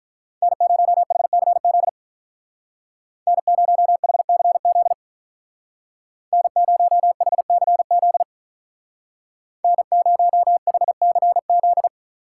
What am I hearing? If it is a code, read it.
Morse code, heard as a keyed tone, about 47 wpm, 700 Hz. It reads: N0HCZ N0HCZ N0HCZ N0HCZ